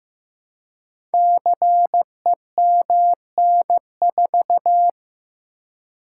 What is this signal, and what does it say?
Morse code, heard as a keyed tone, about 15 wpm, 700 Hz. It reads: CEMN4
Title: Code Groups - 5 Letter:Numbers - 15wpm